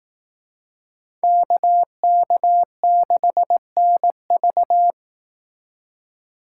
Morse code, heard as a keyed tone, about 18 wpm, 700 Hz. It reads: KK6NV
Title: Speed Racing - USA 2x2 Call Signs 15wpm